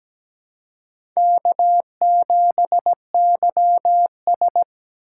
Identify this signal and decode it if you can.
Morse code, heard as a keyed tone, about 17 wpm, 700 Hz. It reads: K7YS
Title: USA Call Signs - 4 Characters - 4x Character Spacing - 17wpm